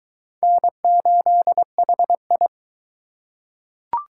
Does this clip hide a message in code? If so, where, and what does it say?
Yes: Morse code from 0.4 to 2.5 s, heard as a keyed tone, about 21 wpm, 700 Hz. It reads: N8HI